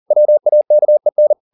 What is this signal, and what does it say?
Morse code, heard as a keyed tone, about 40 wpm, 600 Hz. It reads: WAKEN